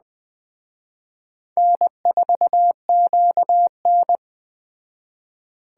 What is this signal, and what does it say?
Morse code, heard as a keyed tone, about 20 wpm, 700 Hz. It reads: N4QN
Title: Speed Racing - USA 1x2 Call Signs 17wpm